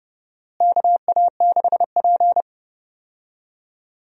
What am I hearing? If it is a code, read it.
Morse code, heard as a keyed tone, about 30 wpm, 700 Hz. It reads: KA6P
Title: USA Call Signs - 4 Characters - Repeated 2x - 30wpm